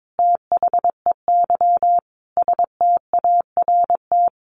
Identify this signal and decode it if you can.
Morse code, heard as a keyed tone, about 22 wpm, 700 Hz. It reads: THEY START
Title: Sentences from Top 1100 Words 22wpm